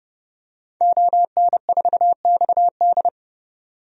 Morse code, heard as a keyed tone, about 32 wpm, 700 Hz. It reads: ON4XD
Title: DXCC-100 Call Signs - 5 Characters - 30wpm